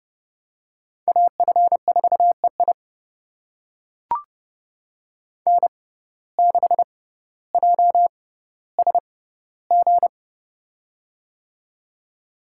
Morse code, heard as a keyed tone, about 30 wpm, spaced as for 15 wpm, 700 Hz. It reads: AF4EI N6JSG
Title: USA Call Signs - 5 Characters - 6x Character Spacing - 30wpm